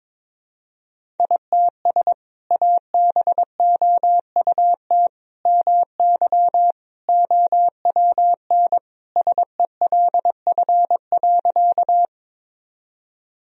Morse code, heard as a keyed tone, about 22 wpm, 700 Hz. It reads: ITS ABOUT MY OWN SELF.
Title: Sentences from Top 1000 Words - Review Part 1 - 22wpm